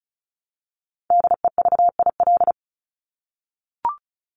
Morse code, heard as a keyed tone, about 34 wpm, 700 Hz. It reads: DEVIL
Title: Top 1751-2000 Words 35wpm